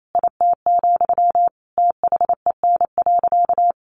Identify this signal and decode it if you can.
Morse code, heard as a keyed tone, about 28 wpm, 700 Hz. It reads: IT, THEN.
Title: Sentences from Top 200 Words 28wpm